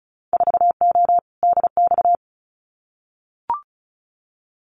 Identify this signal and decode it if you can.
Morse code, heard as a keyed tone, about 34 wpm, 700 Hz. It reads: 4O DX
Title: Winter Field Day Exchanges Encore 35wpm